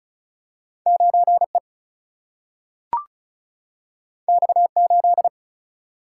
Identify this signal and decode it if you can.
Morse code, heard as a keyed tone, about 36 wpm, 700 Hz. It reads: 9 E X 8
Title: Single-Letter Single-Number 35wpm